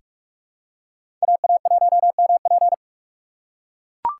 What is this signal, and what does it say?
Morse code, heard as a keyed tone, about 44 wpm, 690 Hz. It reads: AA1MP
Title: USA Call Signs - 5 Characters - 6x Character Spacing - 45wpm